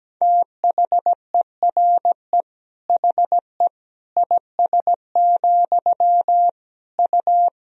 Morse code, heard as a keyed tone, about 17 wpm, 700 Hz. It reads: THERE HE IS, U
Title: Sentences from Top 100 Words and Abbreviations 17wpm